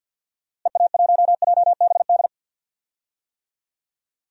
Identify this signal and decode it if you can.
Morse code, heard as a keyed tone, about 47 wpm, 700 Hz. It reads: EA1JBD
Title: DXCC-100 Call Signs - 6 Characters - Repeated 2x - 50wpm